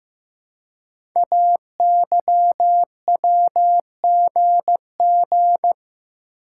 Morse code, heard as a keyed tone, about 15 wpm, 700 Hz. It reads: AYWGG
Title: Code Groups - 5 Letters - 15wpm